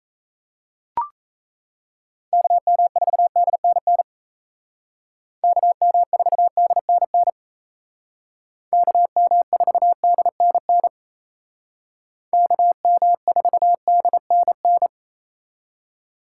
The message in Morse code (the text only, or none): KM4DNN KM4DNN KM4DNN KM4DNN